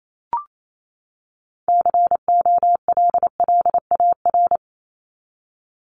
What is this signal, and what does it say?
Morse code, heard as a keyed tone, about 28 wpm, 700 Hz. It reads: COLLAR
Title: Top 1751-2000 Words 28wpm